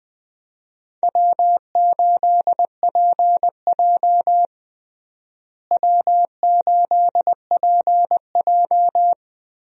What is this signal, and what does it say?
Morse code, heard as a keyed tone, about 20 wpm, 700 Hz. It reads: W8PJ W8PJ
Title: USA Call Signs - 4 Characters - Repeated 3x - 20wpm